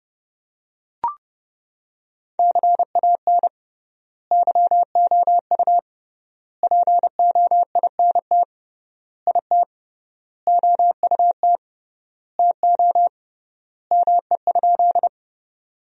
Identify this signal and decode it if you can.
Morse code, heard as a keyed tone, about 31 wpm, 700 Hz. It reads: CAN YOU POINT IT OUT TO ME?
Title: 3x Word-Spacing - Sentences from Top 200 Words - 30wpm